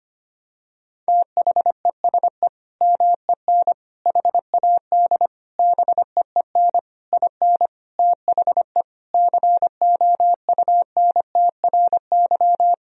Morse code, heard as a keyed tone, about 25 wpm, 700 Hz. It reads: THESE MEN HAD BEEN IN THE COUNTRY
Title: Sentences from Top 1000 Words - Review Part 1 - 25wpm